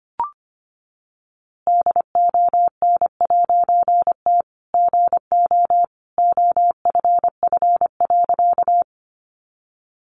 Morse code, heard as a keyed tone, about 25 wpm, 700 Hz. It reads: DON'T GO OFF.
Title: A Sound Of Thunder 25wpm